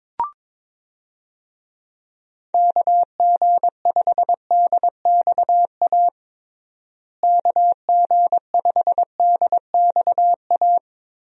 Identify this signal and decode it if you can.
Morse code, heard as a keyed tone, about 22 wpm, 700 Hz. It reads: KG5DXA KG5DXA
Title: USA Call Signs - 6 Characters - Repeated 2x - 22wpm